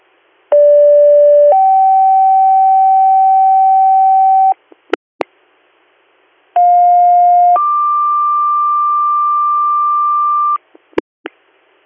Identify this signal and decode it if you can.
Two-tone page, heard as a one-second tone, then a three-second tone, about 580 then 770 Hz, about 710 then 1180 Hz. It is heard from 0.5 to 10.6 s.